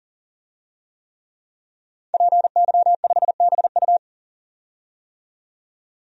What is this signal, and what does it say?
Morse code, heard as a keyed tone, about 40 wpm, 700 Hz. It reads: PY5BU
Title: DXCC-100 Call Signs - 5 Characters - 40wpm